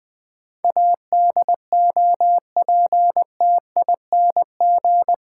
Morse code, heard as a keyed tone, about 20 wpm, 700 Hz. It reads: ADOPTING
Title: Difficult Sentences with Hints - 20wpm